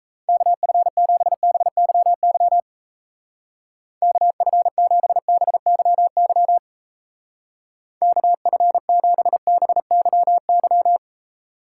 Morse code, heard as a keyed tone, about 36 wpm, 700 Hz. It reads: KF7BYY KF7BYY KF7BYY